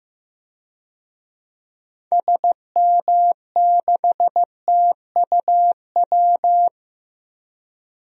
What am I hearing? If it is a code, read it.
Morse code, heard as a keyed tone, about 15 wpm, 700 Hz. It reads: SM6TUW
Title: DXCC-100 Call Signs - 6 Characters - 15wpm